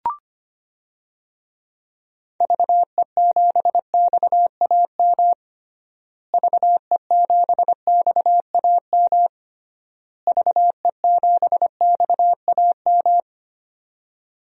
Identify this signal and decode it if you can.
Morse code, heard as a keyed tone, about 25 wpm, 700 Hz. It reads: VE7XAM VE7XAM VE7XAM
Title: DXCC-100 Call Signs - 6 Characters - Repeated 3x - 25wpm